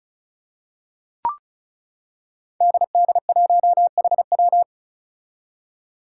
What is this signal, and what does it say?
Morse code, heard as a keyed tone, about 36 wpm, 700 Hz. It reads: DD1HW
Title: Five Challenging Letter-Number Combinations 35wpm